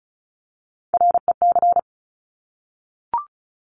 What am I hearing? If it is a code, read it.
Morse code, heard as a keyed tone, about 36 wpm, 700 Hz. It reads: REC
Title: Top Three Letter Combinations in Words 35wpm